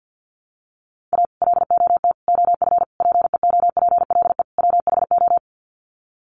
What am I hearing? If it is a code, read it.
Morse code, heard as a keyed tone, about 50 wpm, 700 Hz. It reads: A LOT OF PEOPLE WHO
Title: Sets of 5 Words 50wpm